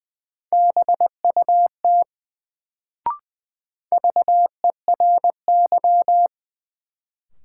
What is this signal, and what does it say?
Morse code, heard as a keyed tone, about 20 wpm, 700 Hz. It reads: BUT VERY
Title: Sets of 2 Words - New Word Drill - 20wpm